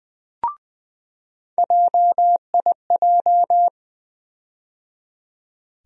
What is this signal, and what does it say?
Morse code, heard as a keyed tone, about 20 wpm, 700 Hz. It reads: JIJ
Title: USA Call Sign Postfixes - 3 Letters 20wpm